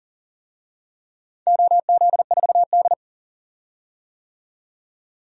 Morse code, heard as a keyed tone, about 40 wpm, 700 Hz. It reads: OZ4D
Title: DXCC-100 Call Signs - 4 Characters - 40wpm